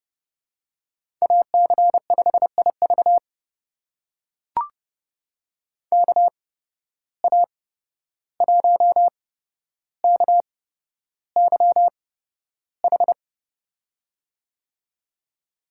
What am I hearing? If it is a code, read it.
Morse code, heard as a keyed tone, about 31 wpm, spaced as for 12 wpm, 700 Hz. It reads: AC5IV KA1KYH
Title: USA Call Signs - 8x Character Spacing - 30wpm